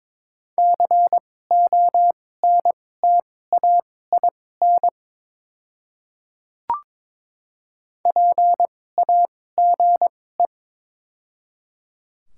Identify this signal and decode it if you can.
Morse code, heard as a keyed tone, about 22 wpm, spaced as for 16 wpm, 700 Hz. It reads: CONTAIN PAGE
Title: 2x Character Spacing - Top 500 Words - 22wpm